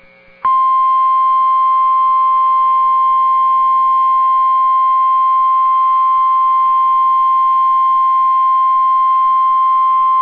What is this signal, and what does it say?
Emergency Alert System (SAME) tones (the 1050 Hz attention tone). It is heard from 0.8 to 9.4 s.